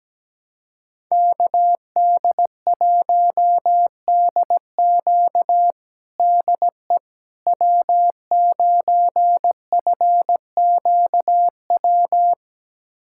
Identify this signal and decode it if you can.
Morse code, heard as a keyed tone, about 17 wpm, 700 Hz. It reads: KD1DQ DE W9FQW
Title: USA Callsign DE Callsign - 5 Characters - 17wpm